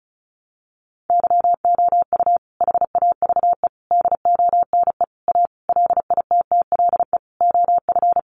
Morse code, heard as a keyed tone, about 34 wpm, 700 Hz. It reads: YOU HAVE DONE A LITTLE OF
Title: Sentences from Top 700 Words 35wpm